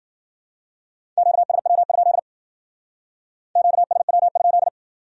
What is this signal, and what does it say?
Morse code, heard as a keyed tone, about 56 wpm, 700 Hz. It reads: QSW? QSW?